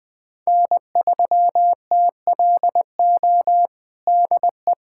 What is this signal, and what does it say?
Morse code, heard as a keyed tone, about 20 wpm, 700 Hz. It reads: N3TLO DE
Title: USA Callsign DE Callsign - 5 Characters - 20wpm